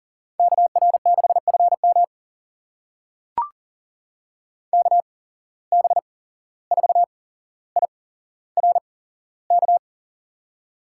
Morse code, heard as a keyed tone, about 40 wpm, spaced as for 16 wpm, 700 Hz. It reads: KR6FM KB4IRK
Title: USA Call Signs - 8x Character Spacing - 40wpm